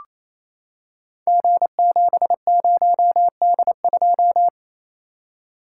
Morse code, heard as a keyed tone, about 28 wpm, 700 Hz. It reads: G70D2
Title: Code Groups - 5 Letter:Numbers - 28wpm